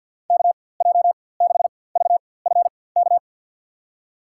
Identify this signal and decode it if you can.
Morse code, heard as a keyed tone, about 46 wpm, spaced as for 23 wpm, 700 Hz. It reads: KJ6VFK
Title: USA Call Signs - 6 Characters - 4x Character Spacing - 50wpm